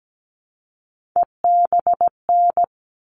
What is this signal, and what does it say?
Morse code, heard as a keyed tone, about 17 wpm, 700 Hz. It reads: EBN
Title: All Cut Numbers - 3-Digit 17wpm